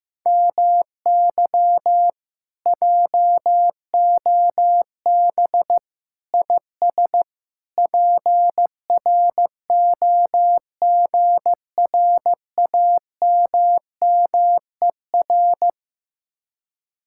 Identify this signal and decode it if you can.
Morse code, heard as a keyed tone, about 15 wpm, 700 Hz. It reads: MY JOB IS PROGRAMMER
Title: Speed-Racing Common QSO Phrases Encore - 15wpm